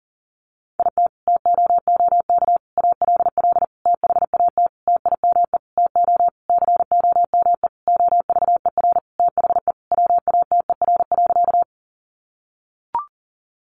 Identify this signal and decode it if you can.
Morse code, heard as a keyed tone, about 37 wpm, 700 Hz. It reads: IT TOOK ALL THAT TIME TO COME OVER THE WATER.
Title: Sentences from Top 300 Words 40wpm